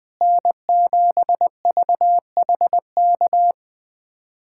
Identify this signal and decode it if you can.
Morse code, heard as a keyed tone, about 20 wpm, 700 Hz. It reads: N7VHK
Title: USA Callsign DE Callsign 20wpm